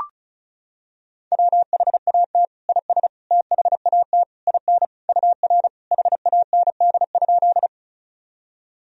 Morse code, heard as a keyed tone, about 36 wpm, 700 Hz. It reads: WHAT IS THAT IN UR HAND?
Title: Sentences from Top 200 Words and Abbreviations 35wpm